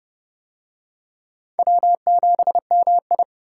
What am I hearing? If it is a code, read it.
Morse code, heard as a keyed tone, about 28 wpm, 700 Hz. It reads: W7MI